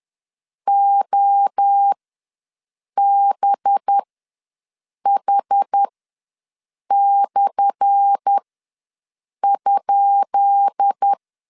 Morse code, heard as a keyed tone, about 10 wpm, spaced as for 6 wpm, 800 Hz. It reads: OBH/?
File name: LEZIONI / 10 SIGNS / 10B INT ws 10wpm 4134